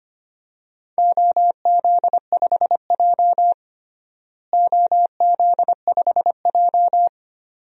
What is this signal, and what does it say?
Morse code, heard as a keyed tone, about 25 wpm, 700 Hz. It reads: OZ5J OZ5J
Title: DXCC-100 Call Signs - 4 Characters - Repeated 2x - 25wpm